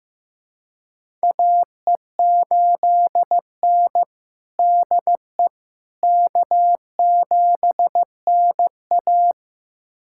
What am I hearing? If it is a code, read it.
Morse code, heard as a keyed tone, about 15 wpm, 700 Hz. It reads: AE8N DE K7NA
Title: USA Callsign DE Callsign - 4 Characters - 15wpm